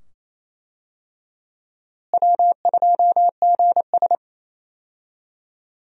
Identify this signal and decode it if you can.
Morse code, heard as a keyed tone, about 28 wpm, 700 Hz. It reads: W2GS
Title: USA Call Signs - 4 Characters - Repeated 2x - 28wpm